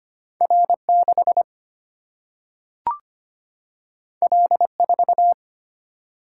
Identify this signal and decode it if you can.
Morse code, heard as a keyed tone, about 25 wpm, 700 Hz. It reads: R 6 L 4
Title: Single-Letter Single-Number 25wpm